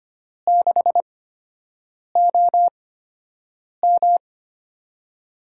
Morse code, heard as a keyed tone, about 25 wpm, spaced as for 7 wpm, 700 Hz. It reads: 6OM
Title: USA Call Signs - 5 Characters - 8x Character Spacing - 25wpm